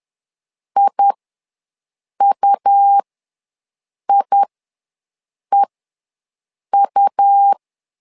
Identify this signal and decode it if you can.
Morse code, heard as a keyed tone, about 11 wpm, spaced as for 6 wpm, 800 Hz. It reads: IUIEU
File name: LEZIONI / 2 UHDN / 2 U ws 10wpm 2017